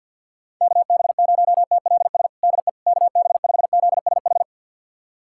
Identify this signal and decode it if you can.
Morse code, heard as a keyed tone, about 50 wpm, 700 Hz. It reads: KB0TLS DE KB5ZSH
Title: USA Callsign DE Callsign - 6 Characters - 50wpm